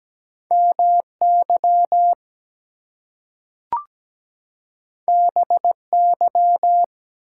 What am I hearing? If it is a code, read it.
Morse code, heard as a keyed tone, about 17 wpm, 700 Hz. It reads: MY BY
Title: Two Letter Words - 17wpm